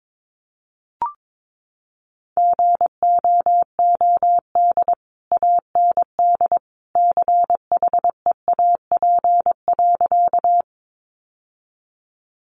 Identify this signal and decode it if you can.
Morse code, heard as a keyed tone, about 22 wpm, 700 Hz. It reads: GOOD AND CHEAP.
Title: Sentences from Top 1200 Words 22wpm